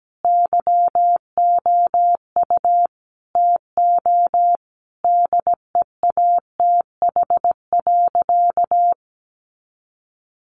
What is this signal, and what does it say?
Morse code, heard as a keyed tone, about 17 wpm, 700 Hz. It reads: YOU TO DEATH.
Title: Sentences from Top 1300 Words 17wpm